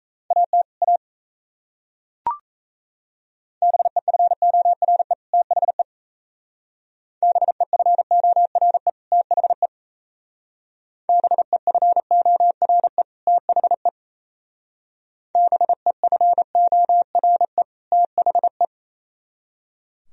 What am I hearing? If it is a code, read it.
Morse code, heard as a keyed tone, about 35 wpm, 690 Hz. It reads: AT A BEFORE THE BEFORE THE BEFORE THE BEFORE THE
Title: Speed-Racing - Sets of 2 Words 28wpm